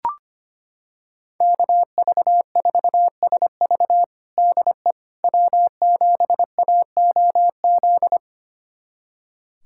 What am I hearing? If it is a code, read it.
Morse code, heard as a keyed tone, about 25 wpm, 700 Hz. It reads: KV4SV DE W7AOZ
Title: USA Callsign DE Callsign - 5 Characters - 25wpm